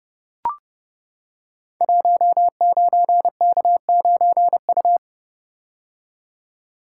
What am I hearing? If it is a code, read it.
Morse code, heard as a keyed tone, about 30 wpm, 700 Hz. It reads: 19K9U
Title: Code Groups - 5 Letter:Numbers - 30wpm